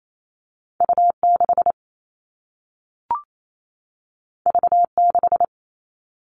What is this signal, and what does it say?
Morse code, heard as a keyed tone, about 28 wpm, 700 Hz. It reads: U6 V6
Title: All Cut Numbers - 2-Digit 28wpm